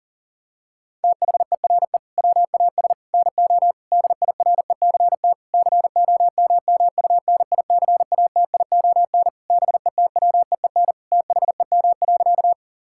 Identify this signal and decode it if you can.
Morse code, heard as a keyed tone, about 40 wpm, 700 Hz. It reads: THERE WAS NO DIRECT COMMUNICATION BETWEEN THEM.
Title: Sentences from Top 1000 Words - Review Part 6 - 40wpm